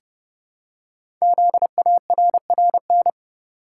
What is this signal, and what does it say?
Morse code, heard as a keyed tone, about 28 wpm, 690 Hz. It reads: ZARRN
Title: Code Groups - 5 Letters - 30wpm